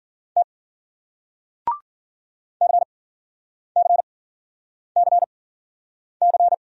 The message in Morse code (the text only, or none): T CCCC